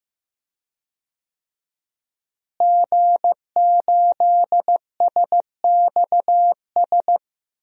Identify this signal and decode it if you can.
Morse code, heard as a keyed tone, about 15 wpm, 700 Hz. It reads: G8SXS
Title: DXCC-100 Call Signs - 5 Characters - Repeated 3x - 15wpm